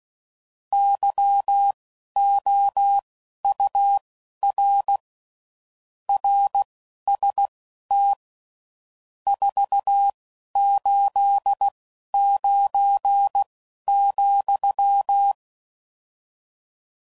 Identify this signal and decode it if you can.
Morse code, heard as a keyed tone, about 16 wpm, spaced as for 11 wpm, 800 Hz. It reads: YOUR RST 489,
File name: CW Code Groups @10WPM / RSTs / Your RST's